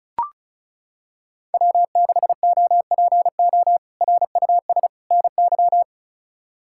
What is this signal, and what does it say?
Morse code, heard as a keyed tone, about 34 wpm, 700 Hz. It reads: W6OPO RUS NY
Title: K1USN SST QSO Practice - 35wpm